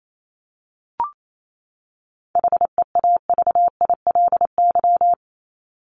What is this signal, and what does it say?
Morse code, heard as a keyed tone, about 28 wpm, 700 Hz. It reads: HEAVILY
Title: LY Suffix - Top 100 Words with Base Words 28wpm